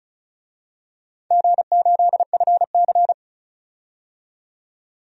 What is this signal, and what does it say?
Morse code, heard as a keyed tone, about 36 wpm, 700 Hz. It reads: G8FC
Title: DXCC-100 Call Signs - 4 Characters - 35wpm